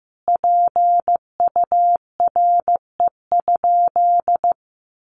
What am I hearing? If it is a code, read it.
Morse code, heard as a keyed tone, about 15 wpm, 700 Hz. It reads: PURE?
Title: Sentences from Top 1500 Words 15wpm